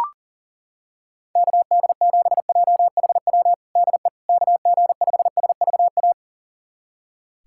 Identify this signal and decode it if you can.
Morse code, heard as a keyed tone, about 40 wpm, 700 Hz. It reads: KD7JHW DE KG5SVA
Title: USA Callsign DE Callsign - 6 Characters - 40wpm